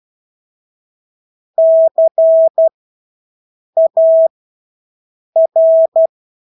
Morse code, heard as a keyed tone, about 12 wpm, spaced as for 6 wpm, 650 Hz. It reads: CAR
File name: Engelske ord 12-6 wpm